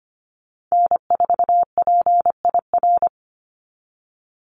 Morse code, heard as a keyed tone, about 25 wpm, 700 Hz. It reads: N4PIR
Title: Call Signs - 25wpm